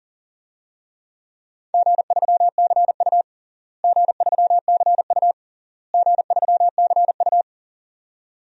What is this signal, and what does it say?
Morse code, heard as a keyed tone, about 37 wpm, 700 Hz. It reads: G3CU G3CU G3CU
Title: DXCC-100 Call Signs - 4 Characters - Repeated 3x - 40wpm